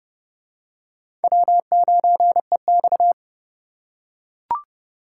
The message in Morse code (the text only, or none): W9EX